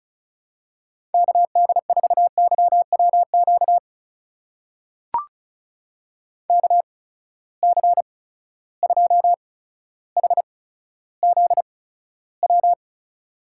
Morse code, heard as a keyed tone, about 36 wpm, spaced as for 15 wpm, 700 Hz. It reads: KD4YWQ KC2HZW
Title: USA Call Signs - 6 Characters - 8x Character Spacing - 35wpm